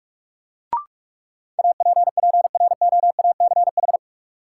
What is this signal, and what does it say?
Morse code, heard as a keyed tone, about 46 wpm, 700 Hz. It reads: APPROACH
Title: Sentences from Top 1400 Words - New Word Drill Alt - 45wpm